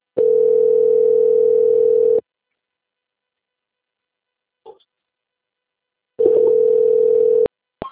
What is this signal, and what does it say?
Ringback tone (North America), 440 + 480 Hz.
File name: out- - - - - .31405